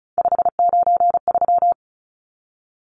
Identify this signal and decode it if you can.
Morse code, heard as a keyed tone, about 34 wpm, 700 Hz. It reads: 593
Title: Numbers - 3-Digit 35wpm